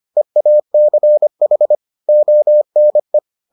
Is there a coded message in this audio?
Yes: Morse code from 0.2 to 3.2 s, heard as a keyed tone, about 25 wpm, 600 Hz. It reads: EACH ONE